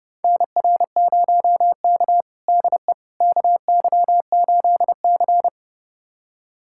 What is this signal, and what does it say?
Morse code, heard as a keyed tone, about 30 wpm, 700 Hz. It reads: NR0K DE KY8C
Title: USA Callsign DE Callsign - 4 Characters - 30wpm